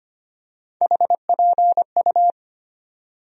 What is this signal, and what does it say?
Morse code, heard as a keyed tone, about 25 wpm, 700 Hz. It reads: HPU
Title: USA Call Sign Postfixes - 3 Letters 25wpm